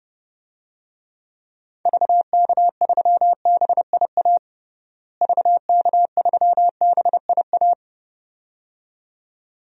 Morse code, heard as a keyed tone, about 30 wpm, 700 Hz. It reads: VK3BIA VK3BIA
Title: DXCC-100 Call Signs - 6 Characters - Repeated 2x - 30wpm